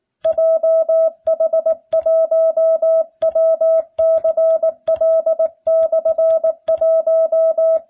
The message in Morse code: JH1WCL/1

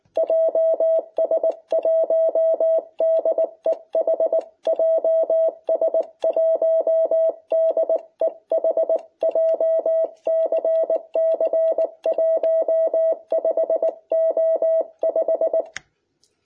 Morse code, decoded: JS1DEH JS1DEH JCC 15O5

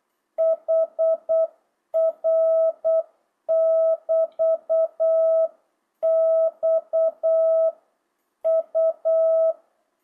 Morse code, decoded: HR <BT> X U

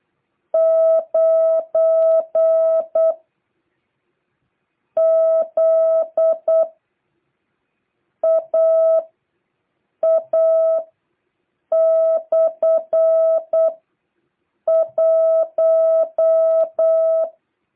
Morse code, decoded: 9ZAA/1

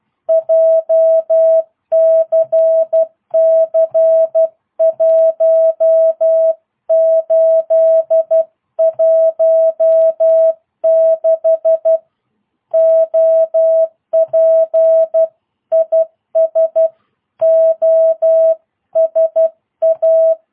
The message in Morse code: JCC 1816 OP IS OSA

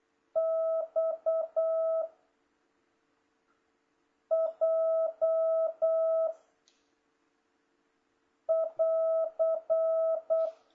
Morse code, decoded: X J <AR>

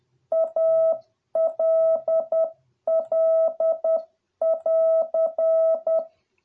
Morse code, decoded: ALL <AR>

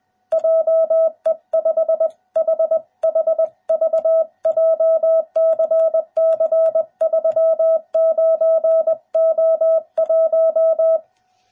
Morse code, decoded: JE5HHV JCC 39O1

